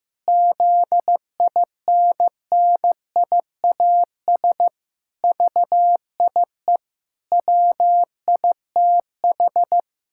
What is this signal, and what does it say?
Morse code, heard as a keyed tone, about 15 wpm, 700 Hz. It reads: ZINNIAS VIE WITH
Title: Pangram Challenge v2 - 15wpm